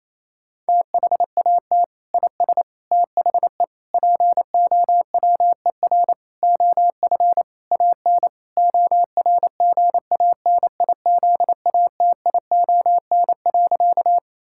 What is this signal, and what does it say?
Morse code, heard as a keyed tone, about 28 wpm, 700 Hz. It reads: THAT IS THE POWER OF AN ORGANIZATION.
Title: Sentences from Top 1000 Words - Review Part 5 - 28wpm